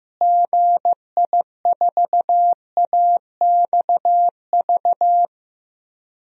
Morse code, heard as a keyed tone, about 15 wpm, 700 Hz. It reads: GI4AXV
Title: DXCC-100 Call Signs - 6 Characters - Repeated 3x - 15wpm